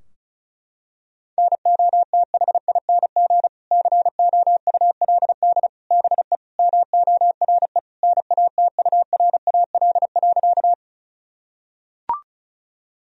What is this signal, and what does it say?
Morse code, heard as a keyed tone, about 36 wpm, 700 Hz. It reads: NOTHING COULD BE MORE NATURAL.
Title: Sentences from Top 1000 Words - Review Part 4 - 35wpm